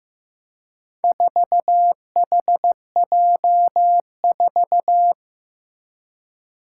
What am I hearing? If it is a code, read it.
Morse code, heard as a keyed tone, about 15 wpm, 700 Hz. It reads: 4HJ4
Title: Four Challenging Letter-Number Combinations 15wpm